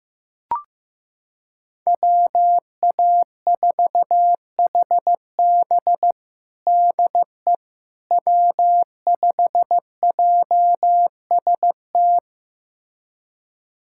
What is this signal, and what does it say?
Morse code, heard as a keyed tone, about 15 wpm, 700 Hz. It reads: WA4HB DE W5JST
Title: USA Callsign DE Callsign - 5 Characters - 15wpm